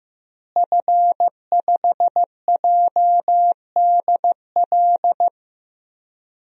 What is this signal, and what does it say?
Morse code, heard as a keyed tone, about 15 wpm, 700 Hz. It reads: F5JDL
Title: DXCC-100 Call Signs - 5 Characters - Repeated 2x - 15wpm